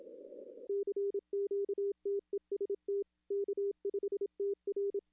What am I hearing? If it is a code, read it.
Morse code, heard as a keyed tone, about 27 wpm, 400 Hz. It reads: CQ TEST K5TR
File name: contests / arrl-160-k5tr-op-w5tn -0200